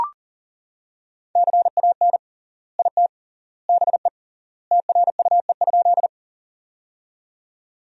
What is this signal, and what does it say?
Morse code, heard as a keyed tone, about 40 wpm, 700 Hz. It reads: CAN IT BE TRUE?
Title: 3x Word-Spacing - Sentences from Top 400 Words - 40wpm